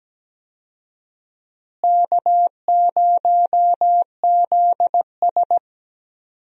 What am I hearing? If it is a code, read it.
Morse code, heard as a keyed tone, about 17 wpm, 700 Hz. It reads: K0ZS
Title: USA Call Signs - 4 Characters - Repeated 2x - 17wpm